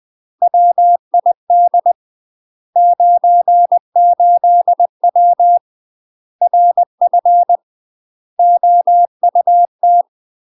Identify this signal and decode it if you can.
Morse code, heard as a keyed tone, about 20 wpm, 700 Hz. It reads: WID 98W RF OUT
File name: UK MASTER QSOs TRAINING / 20WPM / UK MASTER QSOs 0028